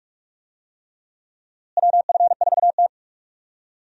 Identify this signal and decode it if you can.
Morse code, heard as a keyed tone, about 48 wpm, 700 Hz. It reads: WF4T
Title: USA Call Signs - 4 Characters - Repeated 3x - 45wpm